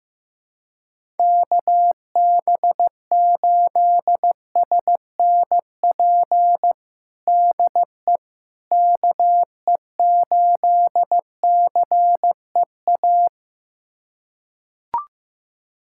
Morse code, heard as a keyed tone, about 15 wpm, 700 Hz. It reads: KB8SNP DE KE8CEA
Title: USA Callsign DE Callsign - 6 Characters - 15wpm